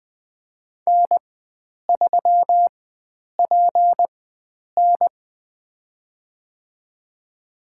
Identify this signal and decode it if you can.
Morse code, heard as a keyed tone, about 21 wpm, spaced as for 9 wpm, 700 Hz. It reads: N3PN
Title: USA Call Signs - 4 Characters - 4x Character Spacing - 20wpm